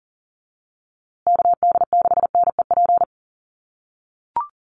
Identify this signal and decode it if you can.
Morse code, heard as a keyed tone, about 40 wpm, 700 Hz. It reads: KD6NEP
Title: USA Call Signs Repeated 3x 40wpm